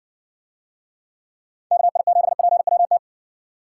Morse code, heard as a keyed tone, about 60 wpm, 700 Hz. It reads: KI7PWT